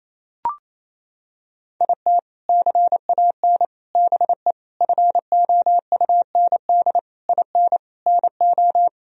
Morse code, heard as a keyed tone, about 28 wpm, 700 Hz. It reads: IT CAN BE FOUND IN NO
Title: Sentences from Top 1000 Words - Review Part 1 - 28wpm